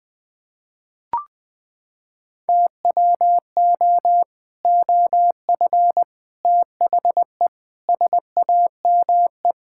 Morse code, heard as a keyed tone, about 20 wpm, 700 Hz. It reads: TWO OF THE SAME
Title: Sentences with 3-Word Phrases - 20wpm